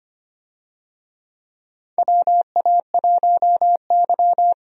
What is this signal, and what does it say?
Morse code, heard as a keyed tone, about 25 wpm, 700 Hz. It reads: WA1Y
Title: USA Call Signs - 4 Characters - Repeated 2x - 25wpm